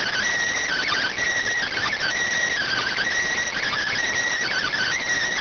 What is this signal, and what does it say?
Emergency Alert System (SAME) tones (header bursts), heard as harsh buzzing bursts. It is heard from 0.6 to 5.0 s.